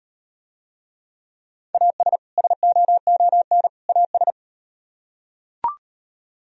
Morse code, heard as a keyed tone, about 38 wpm, 700 Hz. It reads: AS SOON AS